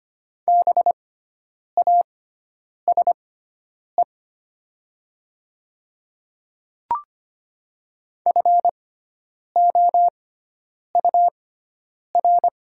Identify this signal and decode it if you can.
Morse code, heard as a keyed tone, about 25 wpm, spaced as for 9 wpm, 700 Hz. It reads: BASE FOUR
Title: 6x Character Spacing - Top 500 Words - 25wpm